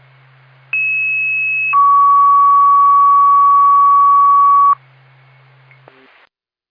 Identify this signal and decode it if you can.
Two-tone page, heard as a one-second tone, then a three-second tone, about 2600 then 1130 Hz. It is heard from 0.7 to 4.7 s.